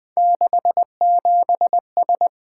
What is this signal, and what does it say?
Morse code, heard as a keyed tone, about 20 wpm, 700 Hz. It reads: 67S
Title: Three Letter-Number Combinations - 20wpm